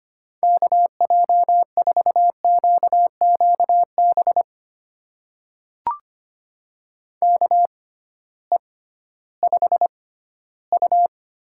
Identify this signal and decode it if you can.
Morse code, heard as a keyed tone, about 25 wpm, spaced as for 15 wpm, 700 Hz. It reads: KJ4QQB KE5U
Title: USA Call Signs - 6 Characters - 6x Character Spacing - 25wpm